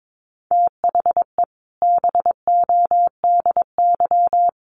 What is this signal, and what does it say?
Morse code, heard as a keyed tone, about 22 wpm, 700 Hz. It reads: THE BODY
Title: A Sound Of Thunder 22wpm